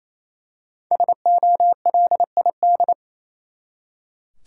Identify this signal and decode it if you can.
Morse code, heard as a keyed tone, about 28 wpm, 700 Hz. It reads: SOLID
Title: Five Letter Words - 28wpm